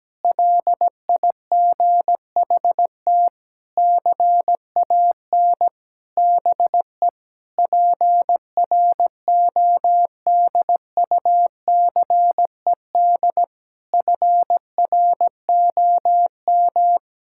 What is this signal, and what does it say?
Morse code, heard as a keyed tone, about 17 wpm, 700 Hz. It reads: LIGHT CAN BE PRODUCED FROM
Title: Sentences from Top 1000 Words - Review Part 6 - 17wpm